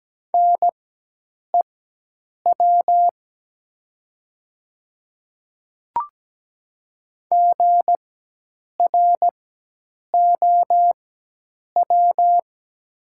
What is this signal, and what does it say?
Morse code, heard as a keyed tone, about 17 wpm, spaced as for 8 wpm, 700 Hz. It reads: NEW GROW
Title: 4x Character Spacing - Top 300 Words - 17wpm